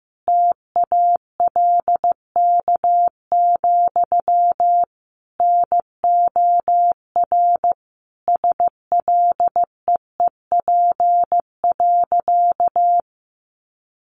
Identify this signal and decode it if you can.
Morse code, heard as a keyed tone, about 15 wpm, 700 Hz. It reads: TALK, NOR SLEEP.
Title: Sentences from Top 1500 Words 15wpm